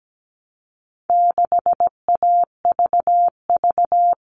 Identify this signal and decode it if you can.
Morse code, heard as a keyed tone, about 17 wpm, 700 Hz. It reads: 6AVV
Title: All Cut Numbers - 4-Digit 17wpm